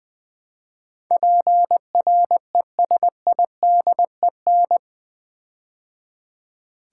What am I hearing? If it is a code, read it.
Morse code, heard as a keyed tone, about 20 wpm, 700 Hz. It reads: PRESIDEN
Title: Build A Word - 4 Letters At A Time - 12-Letter Words - 20wpm